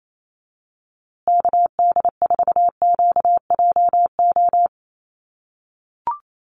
Morse code, heard as a keyed tone, about 28 wpm, 700 Hz. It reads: KD4QJO
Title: USA Call Signs Repeated 2x 28wpm